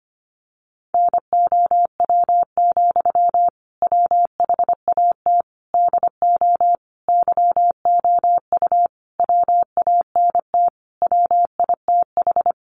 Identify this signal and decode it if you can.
Morse code, heard as a keyed tone, about 25 wpm, 700 Hz. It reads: NOW, WHAT DO YOU WANT WITH
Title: Sentences from Top 200 Words 25wpm